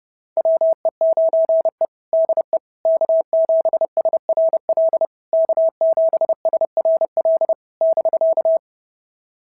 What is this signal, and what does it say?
Morse code, heard as a keyed tone, about 30 wpm, 650 Hz. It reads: WE9E DE K7SRL K7SRL <BK>
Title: SOTA QSO Practice 30wpm